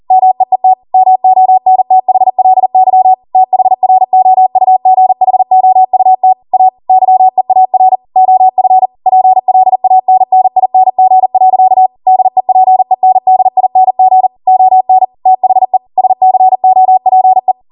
Morse code, heard as a keyed tone, about 40 wpm, 750 Hz. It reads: MEET MONTHLY THROUGHOUT A YEAR OF PLANNING. DEPENDING ON THE SCOPE